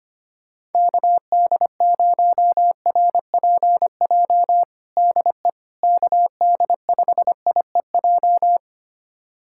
Morse code, heard as a keyed tone, about 25 wpm, 700 Hz. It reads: KD0RPJ DE KD5IEJ
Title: USA Callsign DE Callsign - 6 Characters - 25wpm